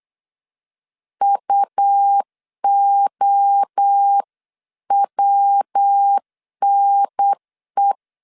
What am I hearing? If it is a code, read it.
Morse code, heard as a keyed tone, about 8 wpm, 800 Hz. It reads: UOWNE